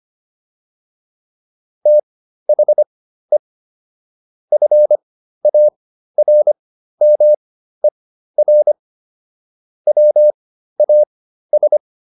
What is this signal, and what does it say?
Morse code, heard as a keyed tone, about 25 wpm, spaced as for 13 wpm, 600 Hz. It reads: THE FARMER WAS